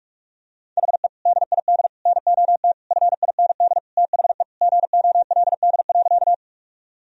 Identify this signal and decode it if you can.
Morse code, heard as a keyed tone, about 46 wpm, 700 Hz. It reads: HE DID NOT FIND THE GOLD.
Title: Sentences from Top 1000 Words - Review Part 3 - 45wpm